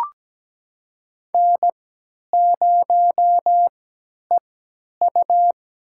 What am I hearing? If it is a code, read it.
Morse code, heard as a keyed tone, about 17 wpm, spaced as for 10 wpm, 700 Hz. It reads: N0EU
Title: USA Call Signs - 3x Character Spacing - 17wpm